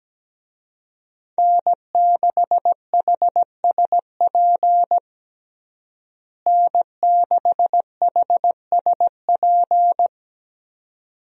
Morse code, heard as a keyed tone, about 17 wpm, 700 Hz. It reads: N6HSP N6HSP
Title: USA Call Signs - 5 Characters - Repeated 2x - 17wpm